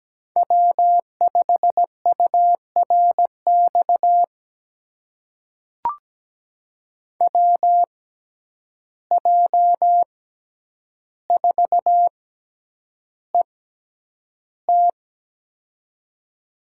Morse code, heard as a keyed tone, about 17 wpm, spaced as for 9 wpm, 700 Hz. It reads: W5URX WJ4ET
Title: USA Call Signs - 5 Characters - 6x Character Spacing - 17wpm